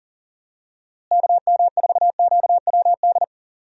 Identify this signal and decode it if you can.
Morse code, heard as a keyed tone, about 38 wpm, 700 Hz. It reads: KM4QWD
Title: USA Call Signs - 6 Characters - Repeated 3x - 40wpm